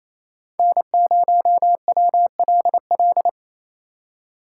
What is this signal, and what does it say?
Morse code, heard as a keyed tone, about 27 wpm, 700 Hz. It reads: N0WLL
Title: USA Call Signs - 2x Character Spacing - 28wpm